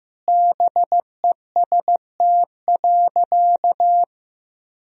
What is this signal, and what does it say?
Morse code, heard as a keyed tone, about 15 wpm, 700 Hz. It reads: BEST.
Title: Sentences from Top 1000 Words - Review Part 3 - 15wpm